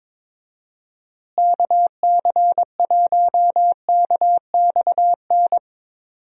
Morse code, heard as a keyed tone, about 22 wpm, 700 Hz. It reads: KC1KXN
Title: USA Call Signs - 6 Characters - Repeated 2x - 22wpm